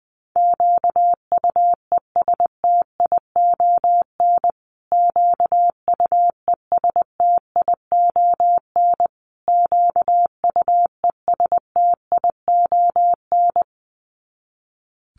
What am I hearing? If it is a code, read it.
Morse code, heard as a keyed tone, about 20 wpm, 700 Hz. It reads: QUESTION QUESTION QUESTION
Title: Top 500 Words Repeated 20wpm